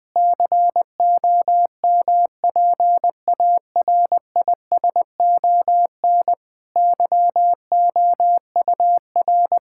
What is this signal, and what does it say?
Morse code, heard as a keyed tone, about 20 wpm, 700 Hz. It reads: COMPARISON YOUR
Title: Sentences from Top 1600 Words 20wpm